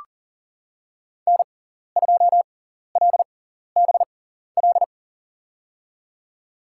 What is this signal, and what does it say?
Morse code, heard as a keyed tone, about 40 wpm, spaced as for 14 wpm, 700 Hz. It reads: N2LBL
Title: USA Call Signs - 5 Characters - 6x Character Spacing - 40wpm